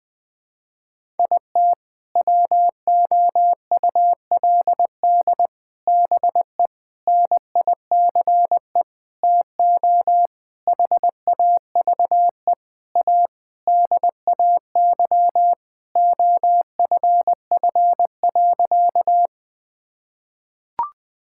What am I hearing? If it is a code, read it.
Morse code, heard as a keyed tone, about 20 wpm, 700 Hz. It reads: IT WOULD BE NICE TO HAVE A DAY OFF.
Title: Sentences with 7-Word Phrases - 20wpm